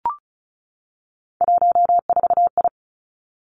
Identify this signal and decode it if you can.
Morse code, heard as a keyed tone, about 36 wpm, 700 Hz. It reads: 14I
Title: Three Challenging Letter-Number Combinations 35wpm